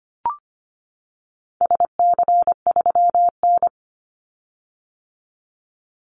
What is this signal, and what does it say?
Morse code, heard as a keyed tone, about 25 wpm, 700 Hz. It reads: SC3N
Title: Call Signs - 25wpm